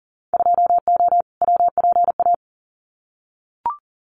Morse code, heard as a keyed tone, about 41 wpm, 700 Hz. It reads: 2O WPA
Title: Winter Field Day Exchanges Encore 40wpm